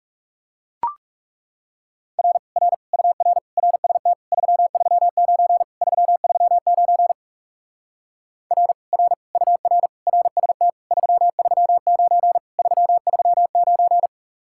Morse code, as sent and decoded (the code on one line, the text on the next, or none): .-. / .-. / ..- .-. / .-. ... - / ...-- ...-- ----. / ...-- ...-- ----. / .-. / .-. / ..- .-. / .-. ... - / ...-- ...-- ----. / ...-- ...-- ----.
R R UR RST 339 339 R R UR RST 339 339